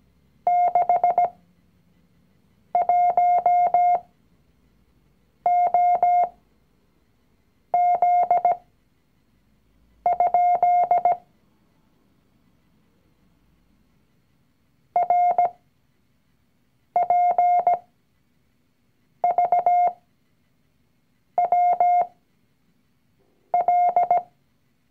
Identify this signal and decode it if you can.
Morse code, heard as a keyed tone, about 17 wpm, spaced as for 5 wpm, 710 Hz. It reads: 61OZ? RPVWL
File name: CW-Online-Kurs / Anfänger / Lektion 019